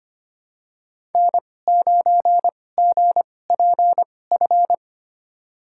Morse code, heard as a keyed tone, about 25 wpm, spaced as for 18 wpm, 700 Hz. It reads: N9GPF
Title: USA Call Signs - 5 Characters - 2x Character Spacing - 25wpm